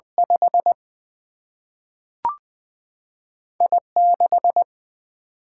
Morse code, heard as a keyed tone, about 21 wpm, 700 Hz. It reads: 5 I6
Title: Single-Letter Single-Number 20wpm